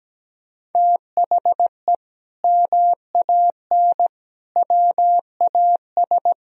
Morse code, heard as a keyed tone, about 17 wpm, 700 Hz. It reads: THE MAN WAS
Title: Sentences from Top 1000 Words - Review Part 4 - 17wpm